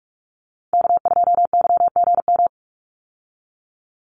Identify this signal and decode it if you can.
Morse code, heard as a keyed tone, about 44 wpm, 700 Hz. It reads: K2YGM
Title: Call Signs Repeated 3x - 45wpm